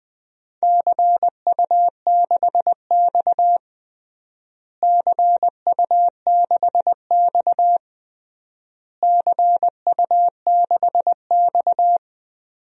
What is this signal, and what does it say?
Morse code, heard as a keyed tone, about 20 wpm, 700 Hz. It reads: CU6X CU6X CU6X
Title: DXCC-100 Call Signs - 4 Characters - Repeated 3x - 20wpm